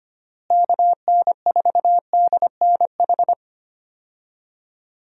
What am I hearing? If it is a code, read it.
Morse code, heard as a keyed tone, about 25 wpm, 700 Hz. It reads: KN4DNH
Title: USA Call Signs - 6 Characters - Repeated 3x - 25wpm